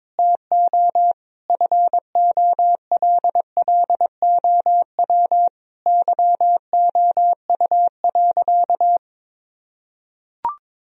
Morse code, heard as a keyed tone, about 22 wpm, 700 Hz. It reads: TO FOLLOW YOU.
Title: Sentences from Top 1000 Words - Review Part 1 - 22wpm